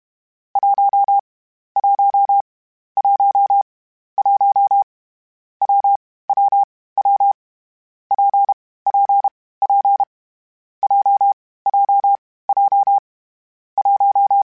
Morse code, heard as a keyed tone, about 32 wpm, 800 Hz. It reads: 1 1 1 1 W W W P P P J J J 1
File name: Variable Speed / Variable Speed-mp3s / WPJ WPJ1 sp @30